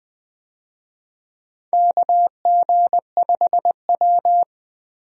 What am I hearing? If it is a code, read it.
Morse code, heard as a keyed tone, about 20 wpm, 700 Hz. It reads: KG5W
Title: USA Call Signs - 4 Characters - Repeated 2x - 20wpm